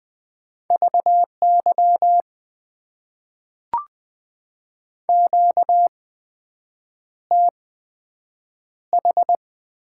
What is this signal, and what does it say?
Morse code, heard as a keyed tone, about 20 wpm, spaced as for 7 wpm, 700 Hz. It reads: VY QTH
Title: Top 90 QSO Elements - 8x Character Spacing - 20wpm